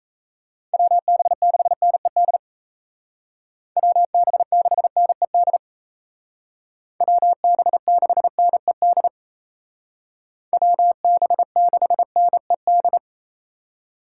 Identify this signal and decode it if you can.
Morse code, heard as a keyed tone, about 35 wpm, 700 Hz. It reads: WB6NED WB6NED WB6NED WB6NED
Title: Speed Racing - USA 2x3 Call Signs 28wpm